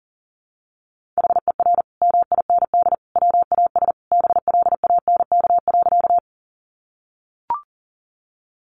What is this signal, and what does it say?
Morse code, heard as a keyed tone, about 40 wpm, 700 Hz. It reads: HER MIND WAS BLANK.
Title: Sentences from Top 1500 Words 40wpm